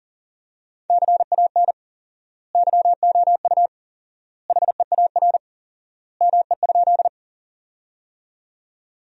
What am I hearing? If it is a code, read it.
Morse code, heard as a keyed tone, about 40 wpm, 700 Hz. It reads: CAN YOU HEAR ME?
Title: 4x Word-Spacing - Sentences from Top 300 Words - 40wpm